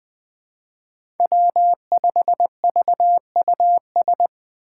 Morse code, heard as a keyed tone, about 20 wpm, 700 Hz. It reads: W5VUS
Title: Five Challenging Letter-Number Combinations 20wpm